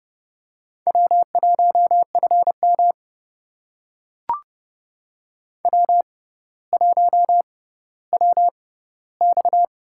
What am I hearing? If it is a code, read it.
Morse code, heard as a keyed tone, about 30 wpm, spaced as for 15 wpm, 700 Hz. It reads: W1FM W1WX
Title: USA Call Signs - 6x Character Spacing - 30wpm